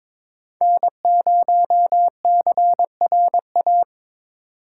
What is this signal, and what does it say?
Morse code, heard as a keyed tone, about 23 wpm, 700 Hz. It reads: N0CRA
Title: USA Call Signs - 4x Character Spacing - 22wpm